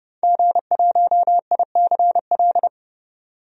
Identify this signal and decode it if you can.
Morse code, heard as a keyed tone, about 30 wpm, 700 Hz. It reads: G1ICL
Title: DXCC-100 Call Signs - 5 Characters - 30wpm